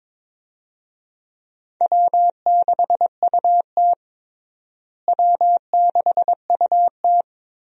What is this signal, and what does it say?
Morse code, heard as a keyed tone, about 22 wpm, 700 Hz. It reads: W6UT W6UT
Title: USA Call Signs - 4 Characters - Repeated 2x - 22wpm